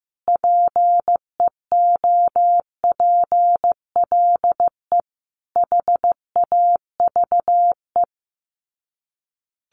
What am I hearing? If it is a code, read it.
Morse code, heard as a keyed tone, about 15 wpm, 700 Hz. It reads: PEOPLE HAVE
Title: Sets of 5 Words 15wpm